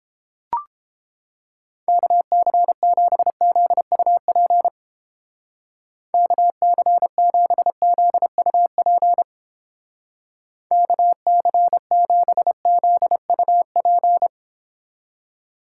Morse code, heard as a keyed tone, about 30 wpm, 700 Hz. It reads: KC7ZUP KC7ZUP KC7ZUP